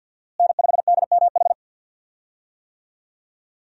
Morse code, heard as a keyed tone, about 49 wpm, 700 Hz. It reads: N5DMH
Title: USA Call Signs - 5 Characters - Repeated 2x - 50wpm